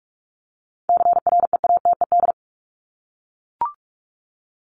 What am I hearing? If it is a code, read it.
Morse code, heard as a keyed tone, about 44 wpm, 700 Hz. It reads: CREATED
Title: ED Suffix - Top 100 Words with Base Words 45wpm